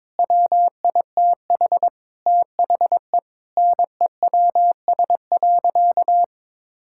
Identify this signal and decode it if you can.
Morse code, heard as a keyed tone, about 22 wpm, 700 Hz. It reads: WITH THE NEWS.
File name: practice / 22wpm / Sentences with 2-Word Phrases - 22wpm